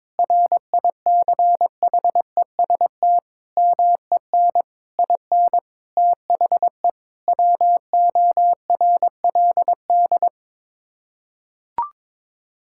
Morse code, heard as a keyed tone, about 22 wpm, 700 Hz. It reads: RICHEST MEN IN THE WORLD
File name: practice / 22wpm / Sets of 8 Words - 22wpm